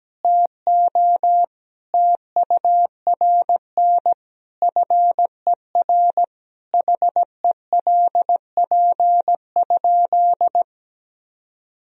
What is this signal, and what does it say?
Morse code, heard as a keyed tone, about 17 wpm, 700 Hz. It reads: TO TURN FER HELP?
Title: Sentences from Top 200 Words and Abbreviations 17wpm